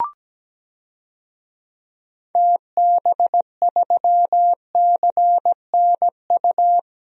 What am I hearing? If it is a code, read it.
Morse code, heard as a keyed tone, about 17 wpm, 700 Hz. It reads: TB3CNU
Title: DXCC-100 Call Signs - 6 Characters - 17wpm